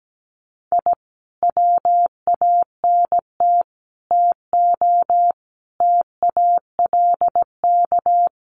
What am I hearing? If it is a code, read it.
Morse code, heard as a keyed tone, about 17 wpm, 700 Hz. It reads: I WANT TO TALK
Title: Sets of 5 Words Encore 17wpm